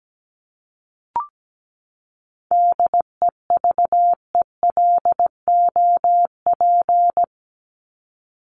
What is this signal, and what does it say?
Morse code, heard as a keyed tone, about 17 wpm, 700 Hz. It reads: DEVELOP
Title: ED Suffix - Top 100 Words with Base Words 17wpm